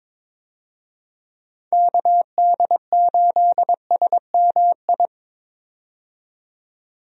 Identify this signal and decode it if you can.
Morse code, heard as a keyed tone, about 22 wpm, 700 Hz. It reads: KD8SMI
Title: USA Call Signs - 6 Characters - 22wpm